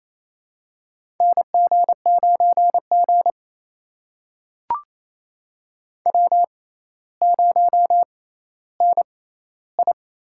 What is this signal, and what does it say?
Morse code, heard as a keyed tone, about 28 wpm, spaced as for 14 wpm, 700 Hz. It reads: NG9G W0NI
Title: USA Call Signs - 4 Characters - 6x Character Spacing - 28wpm